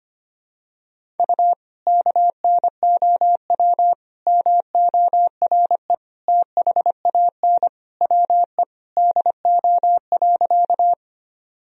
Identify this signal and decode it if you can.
Morse code, heard as a keyed tone, about 25 wpm, 700 Hz. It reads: U KNOW MORE THAN WE DO.
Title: Sentences from Top 100 Words and Abbreviations 25wpm